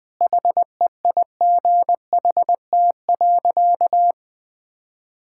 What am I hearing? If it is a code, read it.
Morse code, heard as a keyed tone, about 21 wpm, 700 Hz. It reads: HEIGHT.
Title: Sentences from Top 1600 Words 20wpm